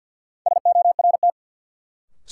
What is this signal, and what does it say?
Morse code, heard as a keyed tone, about 53 wpm, 700 Hz. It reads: SORT
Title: Sets of 2 Words - New Word Drill - 50wpm